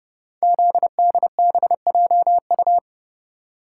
Morse code, heard as a keyed tone, about 30 wpm, 700 Hz. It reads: ZDBJU
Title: Code Groups - 5 Letters - 30wpm